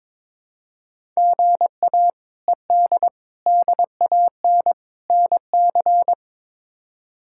Morse code, heard as a keyed tone, about 22 wpm, 700 Hz. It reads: GA ED DAN NC
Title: K1USN SST QSO Practice - 22wpm